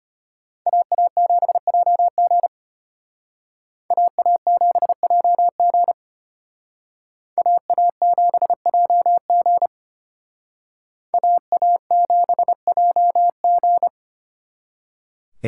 Morse code, heard as a keyed tone, about 32 wpm, 700 Hz. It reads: AA7JG AA7JG AA7JG AA7JG